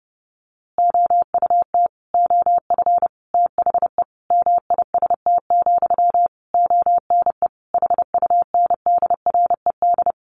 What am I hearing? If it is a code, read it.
Morse code, heard as a keyed tone, about 30 wpm, 700 Hz. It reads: OUT OF THE MIST, ONE HUNDRED
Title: A Sound Of Thunder 30wpm